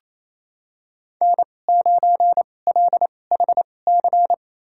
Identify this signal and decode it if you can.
Morse code, heard as a keyed tone, about 28 wpm, spaced as for 20 wpm, 700 Hz. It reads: N9LHC
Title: USA Call Signs - 2x Character Spacing - 28wpm